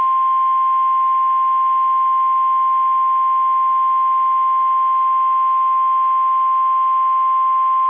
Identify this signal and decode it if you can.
Emergency Alert System (SAME) tones (the 1050 Hz attention tone). It is heard from 0.8 to 7.1 s.
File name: eas / KSDZKDJL / KSDZKDJL